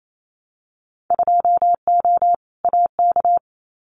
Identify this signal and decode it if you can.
Morse code, heard as a keyed tone, about 28 wpm, 700 Hz. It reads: 2O AK
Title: Winter Field Day Exchanges Encore 28wpm